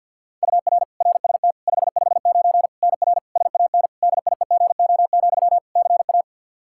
Morse code, heard as a keyed tone, about 50 wpm, 700 Hz. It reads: UR RST 559 NR SAN DIEGO, CA
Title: Practice QSO - Exchange 1 - Beginning - 50wpm